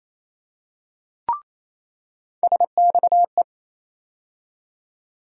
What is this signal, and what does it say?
Morse code, heard as a keyed tone, about 28 wpm, 700 Hz. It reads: SXE
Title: USA Call Sign Postfixes - 3 Letters 28wpm